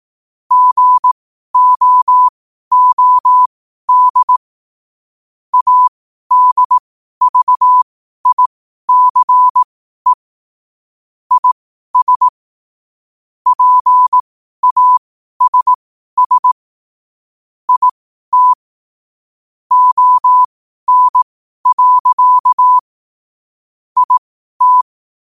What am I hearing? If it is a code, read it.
Morse code, heard as a keyed tone, about 18 wpm, spaced as for 12 wpm, 1000 Hz. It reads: GOOD ADVICE IS PASS IT ON. IT